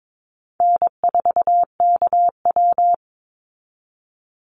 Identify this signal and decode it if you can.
Morse code, heard as a keyed tone, about 22 wpm, 700 Hz. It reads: N4KW
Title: Easy Call Signs Repeated 3x - 22wpm